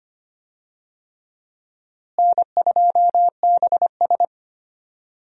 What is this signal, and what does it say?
Morse code, heard as a keyed tone, about 25 wpm, 700 Hz. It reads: N2BS
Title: USA Call Signs - 4 Characters - Repeated 3x - 25wpm